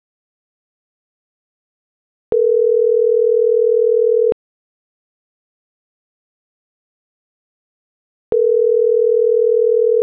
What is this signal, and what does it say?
Ringback tone (North America), 440 + 480 Hz.